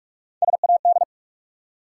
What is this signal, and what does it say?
Morse code, heard as a keyed tone, about 44 wpm, 700 Hz. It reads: SAD
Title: Three Letter Words - 45wpm